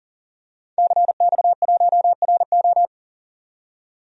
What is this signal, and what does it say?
Morse code, heard as a keyed tone, about 40 wpm, 700 Hz. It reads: CX1RO
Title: DXCC-100 Call Signs - 5 Characters - 40wpm